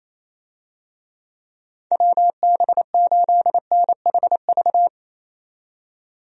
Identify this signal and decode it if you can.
Morse code, heard as a keyed tone, about 27 wpm, 700 Hz. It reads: WB8NHV
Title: USA Call Signs - 6 Characters - 28wpm